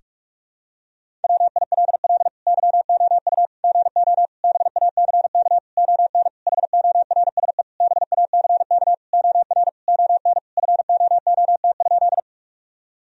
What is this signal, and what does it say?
Morse code, heard as a keyed tone, about 46 wpm, 700 Hz. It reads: WILL YOU GO BACK ON HORSE BACK OR ON FOOT?
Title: Sentences from Top 1000 Words - Review Part 3 - 45wpm